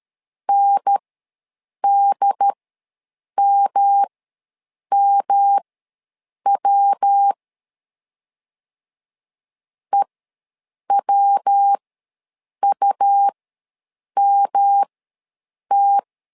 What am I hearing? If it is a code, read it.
Morse code, heard as a keyed tone, about 13 wpm, spaced as for 7 wpm, 800 Hz. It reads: NDMMW EWUMT
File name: LEZIONI / 3 MWRO / 3 MW ws 12wpm 1816